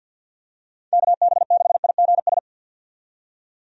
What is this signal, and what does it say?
Morse code, heard as a keyed tone, about 51 wpm, 700 Hz. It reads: KB6IGS
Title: USA Call Signs - 6 Characters - 50wpm